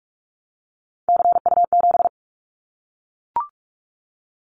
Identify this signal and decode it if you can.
Morse code, heard as a keyed tone, about 44 wpm, 700 Hz. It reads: CU7
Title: Callsign Prefixes for Top 100 DXCC - 45wpm